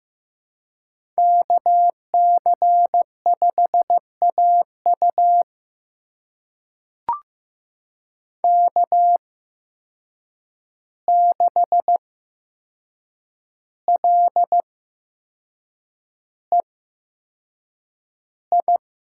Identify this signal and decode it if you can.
Morse code, heard as a keyed tone, about 15 wpm, spaced as for 6 wpm, 700 Hz. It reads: KC5AU K6LEI
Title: USA Call Signs - 5 Characters - 8x Character Spacing - 15wpm